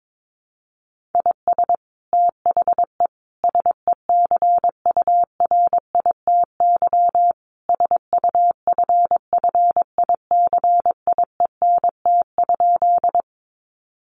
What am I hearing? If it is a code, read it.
Morse code, heard as a keyed tone, about 22 wpm, 700 Hz. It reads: IS THE SECURITY SUFFICIENT?
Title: Sentences from Top 1400 Words 22wpm